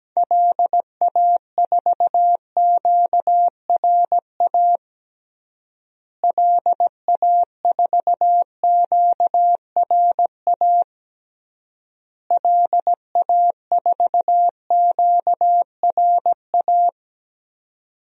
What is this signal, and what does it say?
Morse code, heard as a keyed tone, about 17 wpm, 700 Hz. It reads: LA4QRA LA4QRA LA4QRA
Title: DXCC-100 Call Signs - 6 Characters - Repeated 3x - 17wpm